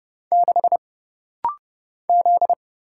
Morse code, heard as a keyed tone, about 30 wpm, 700 Hz. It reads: 6 Z
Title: Single Letter-Number - Rapid-Fire 30wpm